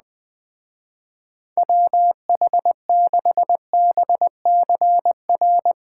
Morse code, heard as a keyed tone, about 20 wpm, 700 Hz. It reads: WH6BCR
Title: Speed Racing - USA 2x3 Call Signs 17wpm